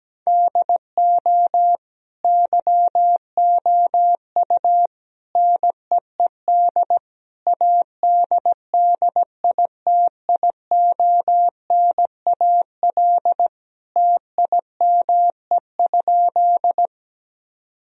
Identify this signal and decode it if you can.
Morse code, heard as a keyed tone, about 17 wpm, 700 Hz. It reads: DO YOU NEED ADDITIONAL TIME?
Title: Sentences from Top 1000 Words - Review Part 4 - 17wpm